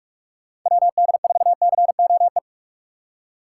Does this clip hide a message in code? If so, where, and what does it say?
Yes: Morse code from 0.7 to 2.4 s, heard as a keyed tone, about 40 wpm, 700 Hz. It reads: WD4COE